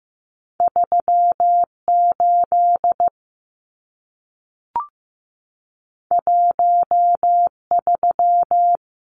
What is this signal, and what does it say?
Morse code, heard as a keyed tone, about 15 wpm, 700 Hz. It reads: 3 8 1 3
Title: Numbers - 2-Digit 15wpm